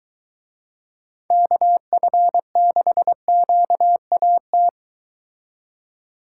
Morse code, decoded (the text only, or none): KF6QAT